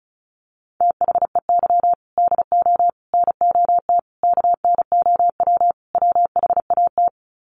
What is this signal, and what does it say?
Morse code, heard as a keyed tone, about 34 wpm, 700 Hz. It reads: THEY DO NOT KNOW WHAT
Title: Sets of 5 Words Encore 35wpm